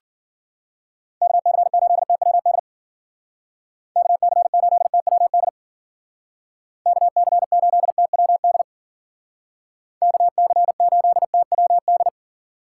Morse code, heard as a keyed tone, about 49 wpm, 700 Hz. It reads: KC8TWD KC8TWD KC8TWD KC8TWD
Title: Speed Racing - USA 2x3 Call Signs 40wpm